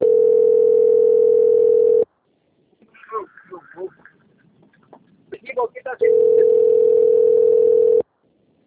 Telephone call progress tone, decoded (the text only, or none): ringback tone